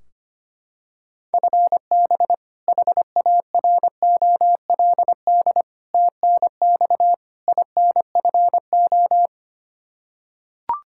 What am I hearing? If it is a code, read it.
Morse code, heard as a keyed tone, about 25 wpm, 700 Hz. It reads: FB HAROLD TNX INFO
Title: Practice QSO - Exchange 3 - More Info - 25wpm